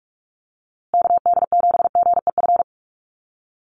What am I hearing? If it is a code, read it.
Morse code, heard as a keyed tone, about 44 wpm, 700 Hz. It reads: KD7GEF
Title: USA Call Signs Repeated 3x 45wpm